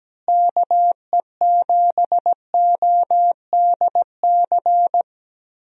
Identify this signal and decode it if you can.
Morse code, heard as a keyed tone, about 17 wpm, 700 Hz. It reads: KE7ODC
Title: USA Call Signs - 6 Characters - Repeated 3x - 17wpm